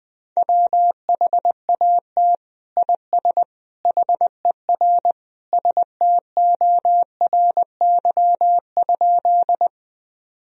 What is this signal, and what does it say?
Morse code, heard as a keyed tone, about 21 wpm, 700 Hz. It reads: WHAT IS HER STORY?
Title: Sentences from Top 1000 Words - Review Part 1 - 20wpm